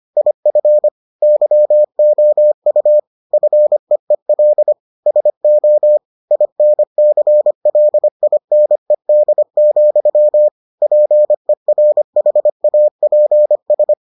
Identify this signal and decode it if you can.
Morse code, heard as a keyed tone, about 25 wpm, 600 Hz. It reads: IF YOU FEEL SO INCLINED, PERHAPS